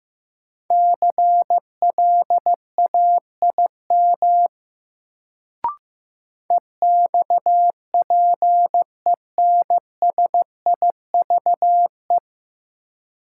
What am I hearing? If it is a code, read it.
Morse code, heard as a keyed tone, about 15 wpm, 700 Hz. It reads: CLAIM EXPENSIVE
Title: Sentences from Top 1500 Words - New Word Drill Alt - 15wpm